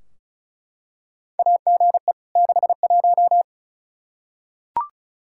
Morse code, heard as a keyed tone, about 36 wpm, 700 Hz. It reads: AGE 61
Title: K1USN SST QSO Practice 35wpm